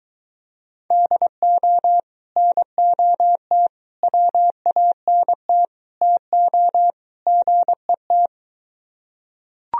Morse code, heard as a keyed tone, about 24 wpm, 700 Hz. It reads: DO NOT WANT TO GET